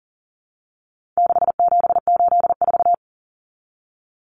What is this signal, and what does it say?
Morse code, heard as a keyed tone, about 40 wpm, 700 Hz. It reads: 6784
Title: Common Cut Numbers - 4-Digit 40wpm